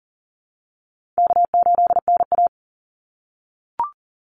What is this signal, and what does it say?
Morse code, heard as a keyed tone, about 40 wpm, 700 Hz. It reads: K8NA
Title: Easy Call Signs - 40wpm